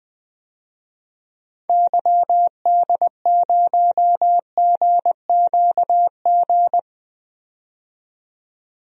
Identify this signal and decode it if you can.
Morse code, heard as a keyed tone, about 20 wpm, 700 Hz. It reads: YD0GQG
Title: DXCC-100 Call Signs - 6 Characters - 20wpm